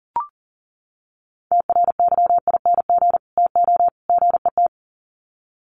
Morse code, heard as a keyed tone, about 40 wpm, 700 Hz. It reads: TRYING TO GET
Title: Sets of 3 Words 40wpm